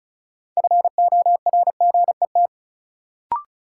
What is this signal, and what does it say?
Morse code, heard as a keyed tone, about 36 wpm, 700 Hz. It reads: FORGET
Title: Sentences from Top 1200 Words - New Word Drill - 35wpm